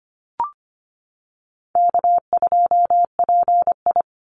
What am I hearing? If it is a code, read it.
Morse code, heard as a keyed tone, about 25 wpm, 700 Hz. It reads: K2PI
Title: Easy Call Signs - 25wpm